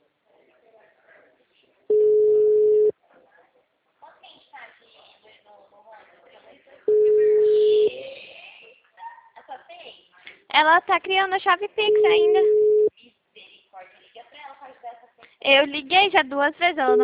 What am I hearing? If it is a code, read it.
Ringback tone (Europe, 425 Hz), with speech.
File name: out- -2002- - - .9223